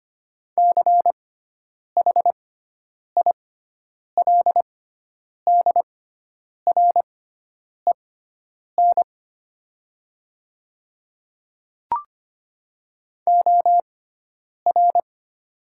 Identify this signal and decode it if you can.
Morse code, heard as a keyed tone, about 25 wpm, spaced as for 9 wpm, 700 Hz. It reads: CHILDREN OR
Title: 6x Character Spacing - Top 500 Words - 25wpm